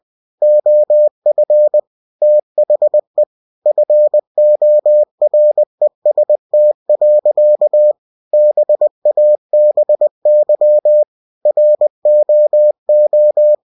Morse code, heard as a keyed tone, about 20 wpm, 600 Hz. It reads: OF THE FOREST. BABY ROO
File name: cw-books / winnie-the-pooh / winnie-20wpm-page-0066